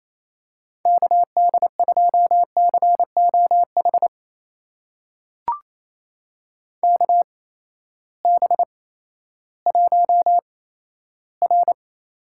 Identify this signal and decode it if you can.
Morse code, heard as a keyed tone, about 28 wpm, spaced as for 14 wpm, 700 Hz. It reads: KD2COH KB1R
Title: USA Call Signs - 6 Characters - 8x Character Spacing - 28wpm